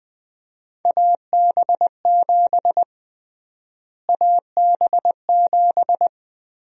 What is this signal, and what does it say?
Morse code, heard as a keyed tone, about 20 wpm, 700 Hz. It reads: AB7 AB7
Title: USA Call Sign Prefixes - Repeated 2x - 20wpm